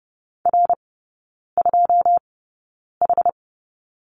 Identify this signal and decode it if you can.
Morse code, heard as a keyed tone, about 30 wpm, spaced as for 9 wpm, 700 Hz. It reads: R2H
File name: 081 Lesson 14e - Number 2 - Three Characters 30wpm